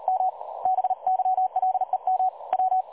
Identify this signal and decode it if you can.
Morse code, heard as a keyed tone, about 38 wpm, 750 Hz. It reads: M BYLEM W
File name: dlazaawansowanych / nagraniazpasma / websdr recording 3565.0kHz